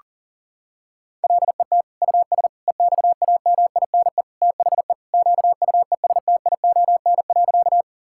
Morse code, heard as a keyed tone, about 40 wpm, 700 Hz. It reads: LET US EXAMINE THE QUESTION.
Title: Sentences from Top 1600 Words 40wpm